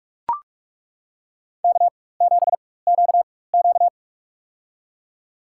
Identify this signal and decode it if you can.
Morse code, heard as a keyed tone, about 42 wpm, spaced as for 20 wpm, 700 Hz. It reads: K7QQ
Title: USA Call Signs - 4 Characters - 4x Character Spacing - 45wpm